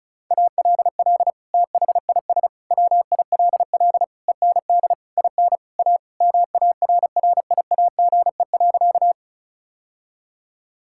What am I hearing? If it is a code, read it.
Morse code, heard as a keyed tone, about 34 wpm, 700 Hz. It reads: ALL THIS WILL END IN A MARRIAGE.
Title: Sentences from Top 1000 Words - Review Part 5 - 35wpm